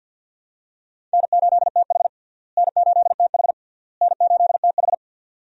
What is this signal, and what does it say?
Morse code, heard as a keyed tone, about 49 wpm, 700 Hz. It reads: N8TH N8TH N8TH
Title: USA Call Signs - 4 Characters - Repeated 3x - 50wpm